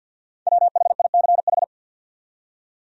Morse code, heard as a keyed tone, about 65 wpm, spaced as for 57 wpm, 700 Hz. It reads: WHICH
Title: Speed-Racing Top 200 Words 50wpm